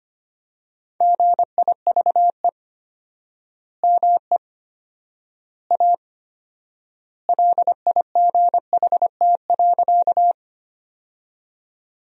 Morse code, heard as a keyed tone, about 25 wpm, 700 Hz. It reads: GIVE ME A LIGHT.
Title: 4x Word-Spacing - Sentences from Top 200 Words - 25wpm